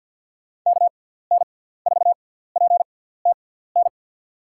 Morse code, heard as a keyed tone, about 52 wpm, spaced as for 17 wpm, 700 Hz. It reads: KN4PTN
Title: USA Call Signs - 6 Characters - 6x Character Spacing - 50wpm